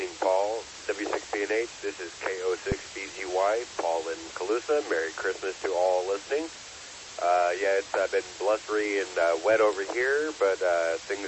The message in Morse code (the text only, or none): none